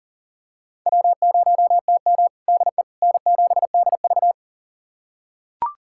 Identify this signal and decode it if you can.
Morse code, heard as a keyed tone, about 40 wpm, 700 Hz. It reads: W0TM DE N7DV
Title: USA Callsign DE Callsign - 4 Characters - 40wpm